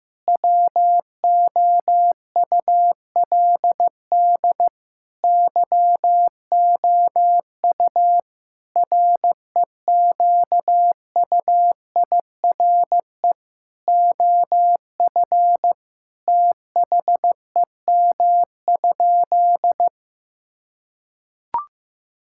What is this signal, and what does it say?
Morse code, heard as a keyed tone, about 15 wpm, 700 Hz. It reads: WOULD YOU REQUIRE OF THEM?
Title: Sentences from Top 1000 Words - Review Part 6 - 15wpm